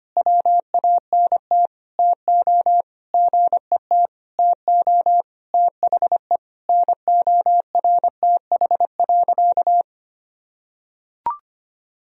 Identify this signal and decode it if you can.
Morse code, heard as a keyed tone, about 25 wpm, 700 Hz. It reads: WANT TO GET TO THE NORTH.
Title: Sentences from Top 1000 Words - Review Part 1 - 25wpm